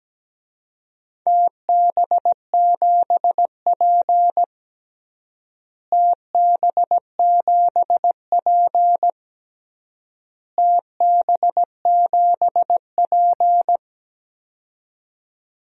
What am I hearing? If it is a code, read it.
Morse code, heard as a keyed tone, about 17 wpm, 700 Hz. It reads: TB7P TB7P TB7P
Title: DXCC-100 Call Signs - 4 Characters - Repeated 3x - 17wpm